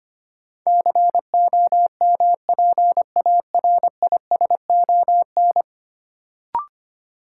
Morse code, heard as a keyed tone, about 25 wpm, 700 Hz. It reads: COMPARISON
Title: Sentences from Top 1600 Words - New Word Drill - 25wpm